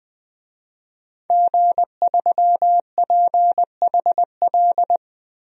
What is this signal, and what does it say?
Morse code, heard as a keyed tone, about 20 wpm, 700 Hz. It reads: G3PHL
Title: DXCC-100 Call Signs - 5 Characters - 20wpm